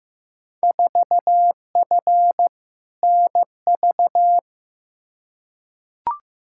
Morse code, heard as a keyed tone, about 15 wpm, 700 Hz. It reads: 4F NV
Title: Field Day Exchanges Encore - 15wpm